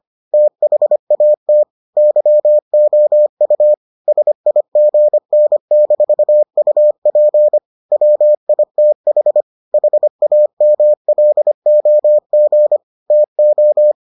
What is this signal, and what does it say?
Morse code, heard as a keyed tone, about 25 wpm, 600 Hz. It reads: THAT YOU SIGN-UP WITH HAMLOG TO